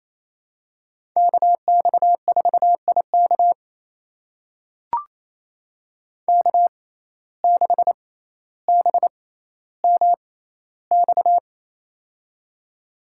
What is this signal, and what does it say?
Morse code, heard as a keyed tone, about 28 wpm, spaced as for 14 wpm, 700 Hz. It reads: KX4IK K6BMX
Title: USA Call Signs - 5 Characters - 6x Character Spacing - 28wpm